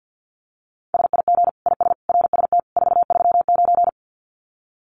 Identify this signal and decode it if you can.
Morse code, heard as a keyed tone, about 50 wpm, 700 Hz. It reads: SIG IS RST 439
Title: Common QSO Phrases 50wpm